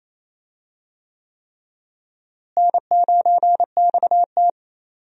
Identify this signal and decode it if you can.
Morse code, heard as a keyed tone, about 28 wpm, 700 Hz. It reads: N9XT
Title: USA Call Signs - 4 Characters - Repeated 3x - 28wpm